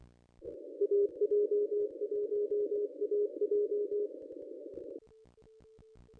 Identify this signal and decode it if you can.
Morse code, heard as a keyed tone, about 23 wpm, 410 Hz. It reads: AJ1AJ
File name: contests / ss-cw-k5tr-op-n5rz -1000